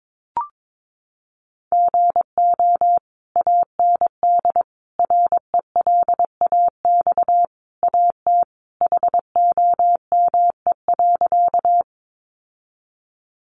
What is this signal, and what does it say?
Morse code, heard as a keyed tone, about 22 wpm, 700 Hz. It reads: GO AND RELAX AT HOME.
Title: Sentences from Top 1400 Words 22wpm